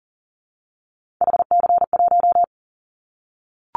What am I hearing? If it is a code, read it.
Morse code, heard as a keyed tone, about 40 wpm, 700 Hz. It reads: HC1
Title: Callsign Prefixes for Top 100 DXCC - 40wpm